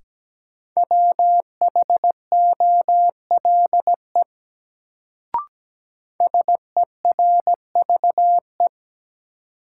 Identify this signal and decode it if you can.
Morse code, heard as a keyed tone, about 17 wpm, 700 Hz. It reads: WHOLE SERVE
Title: Sentences from Top 400 Words - New Word Drill - 17wpm